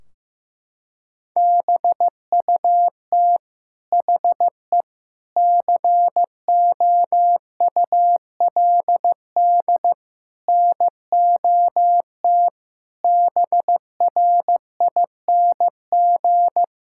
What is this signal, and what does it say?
Morse code, heard as a keyed tone, about 15 wpm, 700 Hz. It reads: BUT HE COULD NOT BRING
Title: Sentences from Top 1600 Words 15wpm